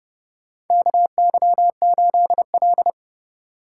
Morse code, heard as a keyed tone, about 31 wpm, 700 Hz. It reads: KY8L
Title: USA Call Signs - 4 Characters - 4x Character Spacing - 30wpm